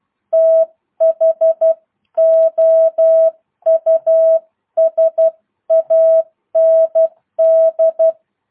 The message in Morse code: THOUSAND

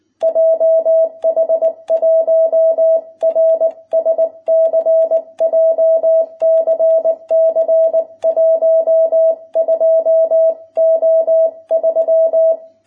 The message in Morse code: JH1RSC JCC 12O3

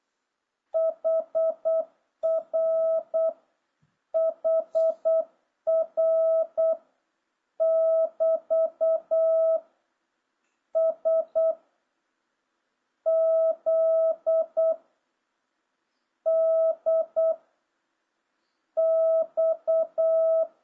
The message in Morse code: HR HR <BT> S Z D X

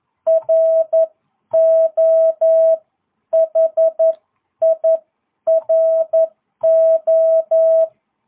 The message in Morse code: RO HIRO

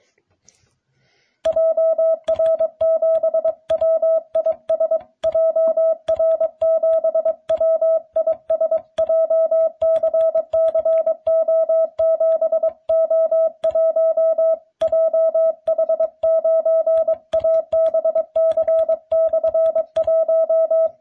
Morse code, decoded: JR7WIS JR7WIS JCCO7O1 JH9ABC/1

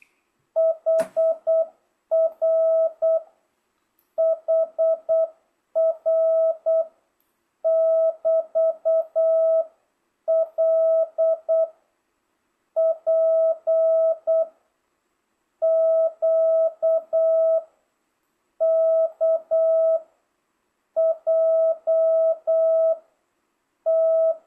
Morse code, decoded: HR HR <BT> L P Q K J T

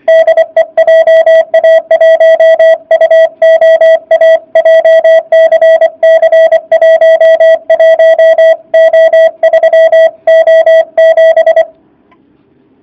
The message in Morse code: DE JA1UOA JCC 11O3O7

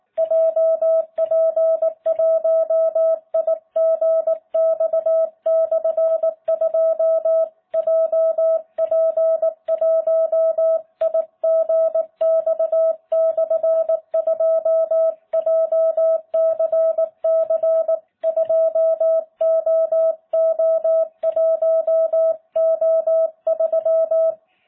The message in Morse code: JP1IGX/2 JP1IGX/2 JCC2OO1O3

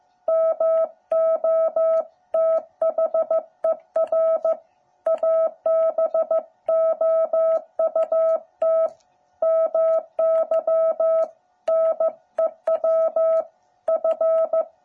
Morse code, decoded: MOTHER ABOUT MY NEW F

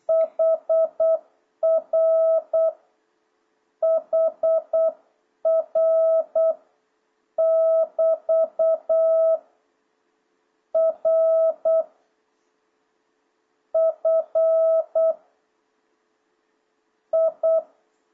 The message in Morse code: HR HR <BT> R F I

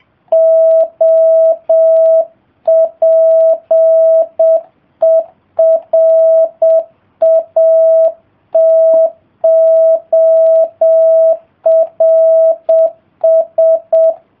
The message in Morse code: OPERATORS

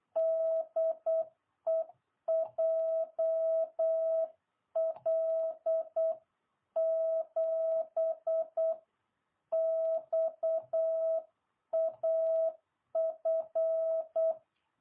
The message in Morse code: DE JL7XAF

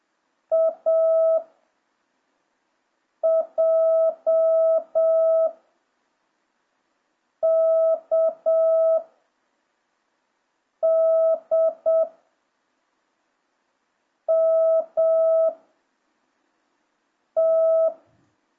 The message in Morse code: A J K D M T

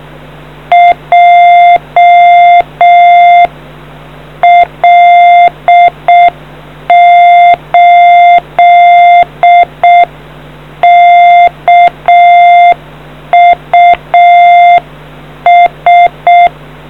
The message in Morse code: JL8KUS

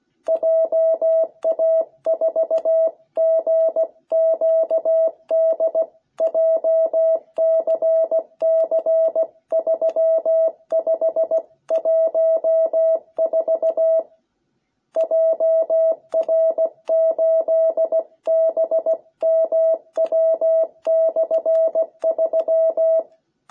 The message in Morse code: JA4GQD JCC3514 JR8BMW/3

